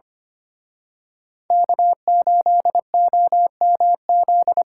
Morse code, heard as a keyed tone, about 28 wpm, 700 Hz. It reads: K8OMZ
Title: Speed Racing - USA 1x3 Call Signs 25wpm